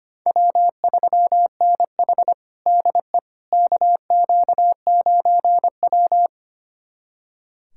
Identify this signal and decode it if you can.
Morse code, heard as a keyed tone, about 25 wpm, 700 Hz. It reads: W3NH DE KQ9W
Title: USA Callsign DE Callsign - 4 Characters - 25wpm